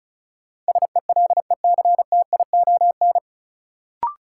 Morse code, heard as a keyed tone, about 34 wpm, 700 Hz. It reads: SELECTION
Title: Sentences from Top 1400 Words - New Word Drill Alt - 35wpm